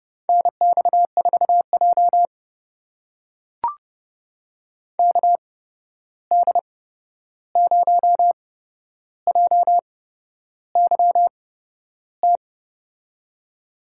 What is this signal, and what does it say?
Morse code, heard as a keyed tone, about 30 wpm, spaced as for 11 wpm, 700 Hz. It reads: NX4J KD0JYT
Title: USA Call Signs - 8x Character Spacing - 30wpm